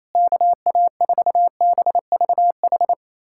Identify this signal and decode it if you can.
Morse code, heard as a keyed tone, about 28 wpm, 700 Hz. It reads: KA4BVH
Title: USA Callsign DE Callsign - 6 Characters - 28wpm